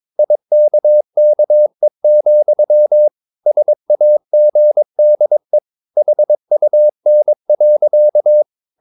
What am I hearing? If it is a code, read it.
Morse code, heard as a keyed tone, about 22 wpm, 600 Hz. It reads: IKKE, SAGDE HUN.
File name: Ferie QRS 22wpm